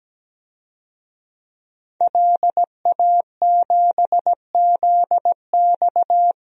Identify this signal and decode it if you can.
Morse code, heard as a keyed tone, about 17 wpm, 700 Hz. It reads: LA7ZX
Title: DXCC-100 Call Signs - 5 Characters - Repeated 2x - 17wpm